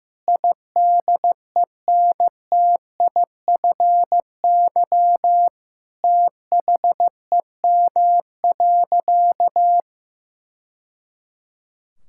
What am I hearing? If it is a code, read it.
Morse code, heard as a keyed tone, about 16 wpm, 700 Hz. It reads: IDENTIFY THEM.
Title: Sentences from Top 1000 Words - Review Part 6 - 15wpm